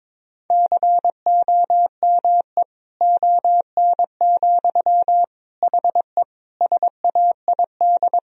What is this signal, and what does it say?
Morse code, heard as a keyed tone, about 22 wpm, 700 Hz. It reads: COME ON, HE SAID
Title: Sentences from Top 1000 Words - Review Part 2 - 22wpm